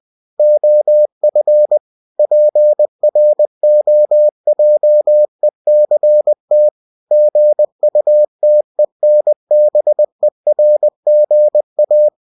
Morse code, heard as a keyed tone, about 20 wpm, 600 Hz. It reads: OF PROJECT GUTENBERGA
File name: cw-books / winnie-the-pooh / winnie-20wpm-page-0114